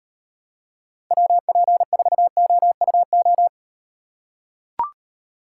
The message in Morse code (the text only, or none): WP4OUO